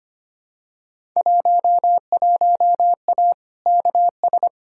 Morse code, heard as a keyed tone, about 25 wpm, 700 Hz. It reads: 11A KS
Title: Field Day Exchanges Encore - 25wpm